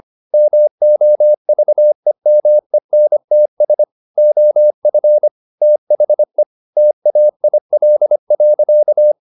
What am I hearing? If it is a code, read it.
Morse code, heard as a keyed tone, about 25 wpm, 600 Hz. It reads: MOVEMENTS OF THE TAIL.